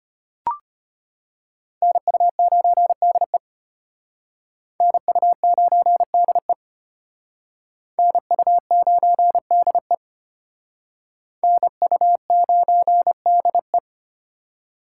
Morse code, heard as a keyed tone, about 33 wpm, 700 Hz. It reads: NU9DE NU9DE NU9DE NU9DE